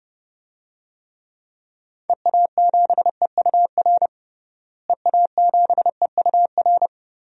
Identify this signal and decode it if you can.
Morse code, heard as a keyed tone, about 30 wpm, 700 Hz. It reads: EA7EUR EA7EUR
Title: DXCC-100 Call Signs - 6 Characters - Repeated 2x - 30wpm